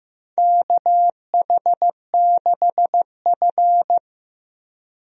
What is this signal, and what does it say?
Morse code, heard as a keyed tone, about 15 wpm, 700 Hz. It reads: KH6F
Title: DXCC-100 Call Signs - 4 Characters - Repeated 2x - 15wpm